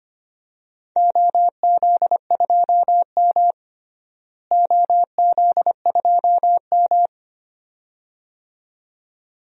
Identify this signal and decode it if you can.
Morse code, heard as a keyed tone, about 25 wpm, 700 Hz. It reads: OZ2M OZ2M
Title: DXCC-100 Call Signs - 4 Characters - Repeated 2x - 25wpm